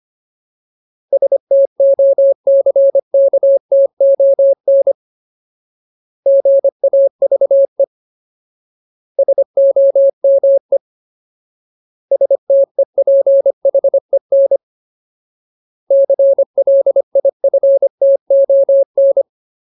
Morse code, heard as a keyed tone, about 25 wpm, 550 Hz. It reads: STOCKTON GAVE SOME STEPHEN CLIFTON
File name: Ultimate Word List / 25WPM Ultimate